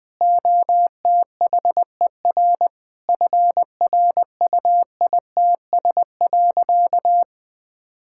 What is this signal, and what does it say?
Morse code, heard as a keyed tone, about 20 wpm, 700 Hz. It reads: OTHER FRUITS.
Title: Sentences with 7-Word Phrases - Spoken Beginning - 20wpm